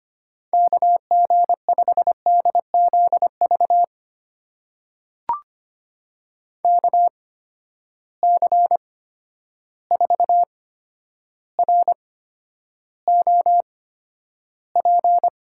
Morse code, heard as a keyed tone, about 25 wpm, spaced as for 11 wpm, 700 Hz. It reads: KG5DZV KC4ROP
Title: USA Call Signs - 6 Characters - 8x Character Spacing - 25wpm